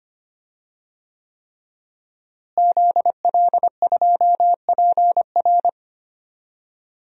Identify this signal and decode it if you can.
Morse code, heard as a keyed tone, about 25 wpm, 700 Hz. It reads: ZL2PR
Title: DXCC-100 Call Signs - 5 Characters - Repeated 3x - 25wpm